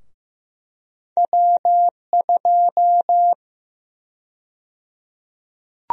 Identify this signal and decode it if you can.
Morse code, heard as a keyed tone, about 15 wpm, 700 Hz. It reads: W2
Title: USA Call Sign Prefixes - 15wpm